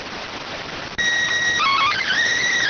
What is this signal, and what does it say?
SSTV (slow-scan television), PD 290.